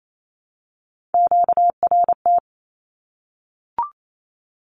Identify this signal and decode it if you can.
Morse code, heard as a keyed tone, about 28 wpm, 700 Hz. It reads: QRT
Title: Q-Codes Repeated 3x 28wpm